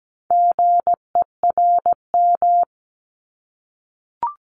Morse code, heard as a keyed tone, about 17 wpm, 700 Hz. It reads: GERM
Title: Multi-letter Phonemes 17wpm